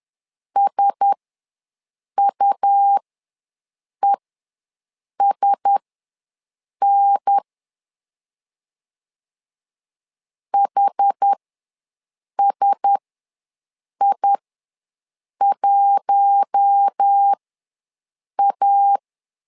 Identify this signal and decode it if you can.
Morse code, heard as a keyed tone, about 10 wpm, spaced as for 6 wpm, 800 Hz. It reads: SUESN HSI1A
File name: LEZIONI / 4 Q1S ws 10wpm 2208